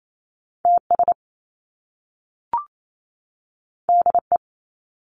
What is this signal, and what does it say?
Morse code, heard as a keyed tone, about 28 wpm, 700 Hz. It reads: TS DE
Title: Top Two Letter Combinations in Words 28wpm